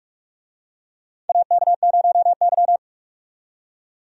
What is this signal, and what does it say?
Morse code, heard as a keyed tone, about 46 wpm, 700 Hz. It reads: AK0Y
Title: USA Call Signs - 4 Characters - 45wpm